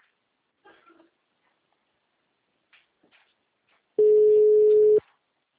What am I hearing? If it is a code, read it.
Ringback tone (Europe, 425 Hz).